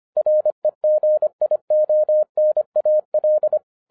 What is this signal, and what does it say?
Morse code, heard as a keyed tone, about 25 wpm, 610 Hz. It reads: REGIONAL
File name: 2012-01-mp3 / n06-FIRST FISTS CONTACT-25wpm